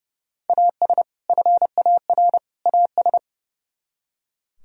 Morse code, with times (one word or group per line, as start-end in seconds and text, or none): 0.5-1.0 AS
1.3-2.4 FAR
2.7-3.2 AS